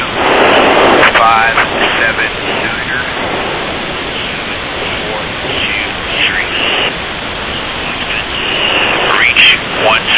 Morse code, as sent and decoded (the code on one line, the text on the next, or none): none